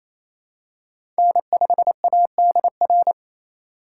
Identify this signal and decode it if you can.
Morse code, heard as a keyed tone, about 28 wpm, 700 Hz. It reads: N5ADR
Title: USA Call Signs - 5 Characters - 4x Character Spacing - 28wpm